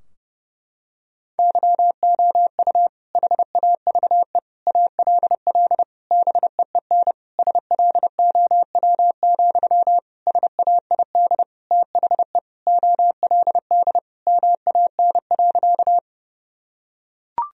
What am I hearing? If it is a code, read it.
Morse code, heard as a keyed tone, about 30 wpm, 700 Hz. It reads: YOU HAVE ALL BEEN SLOW, SAID THE OLD MAN.
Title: Sentences from Top 1000 Words - Review Part 2 - 30wpm